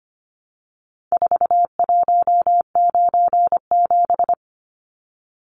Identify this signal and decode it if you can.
Morse code, heard as a keyed tone, about 25 wpm, 700 Hz. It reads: 4197
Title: Numbers - 4-Digit 25wpm